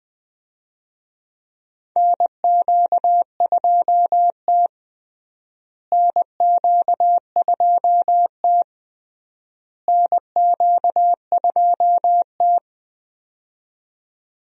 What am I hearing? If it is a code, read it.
Morse code, heard as a keyed tone, about 20 wpm, 700 Hz. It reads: NQ2T NQ2T NQ2T
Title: USA Call Signs - 4 Characters - Repeated 3x - 20wpm